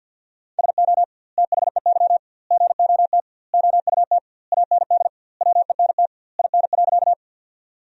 Morse code, heard as a keyed tone, about 53 wpm, 700 Hz. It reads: SO THEY GOT OUT AND WENT IN.
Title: 2x Word-Spacing - Sentences from Top 300 Words - 50wpm